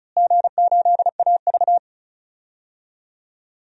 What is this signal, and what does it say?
Morse code, heard as a keyed tone, about 34 wpm, 700 Hz. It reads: G8AV
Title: DXCC-100 Call Signs - 4 Characters - Repeated 2x - 35wpm